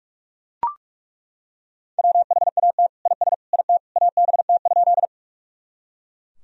Morse code, heard as a keyed tone, about 42 wpm, 700 Hz. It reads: WHAT IS IT ABT?
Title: Sentences from Top 100 Words and Abbreviations 45wpm